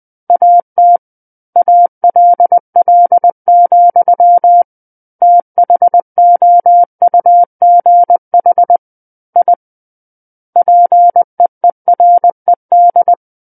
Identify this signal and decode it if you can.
Morse code, heard as a keyed tone, about 20 wpm, 700 Hz. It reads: AT ALL, THOUGH I PEERED
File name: 20 WPM